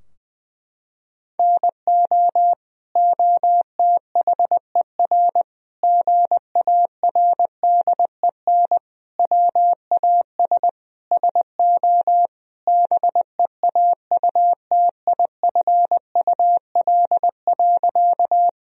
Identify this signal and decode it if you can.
Morse code, heard as a keyed tone, about 20 wpm, 700 Hz. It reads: NO OTHER GARDEN WAS SO BEAUTIFUL.
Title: Sentences from Top 1000 Words - Review Part 4 - 20wpm